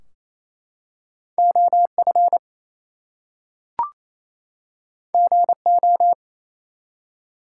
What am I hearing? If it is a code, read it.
Morse code, heard as a keyed tone, about 29 wpm, 700 Hz. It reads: OF GO
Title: Two Letter Words - 28wpm